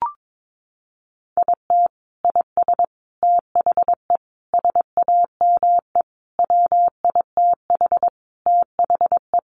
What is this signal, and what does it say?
Morse code, heard as a keyed tone, about 22 wpm, 700 Hz. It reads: IT IS THE SAME WITH THE
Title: Sentences from Top 1100 Words 22wpm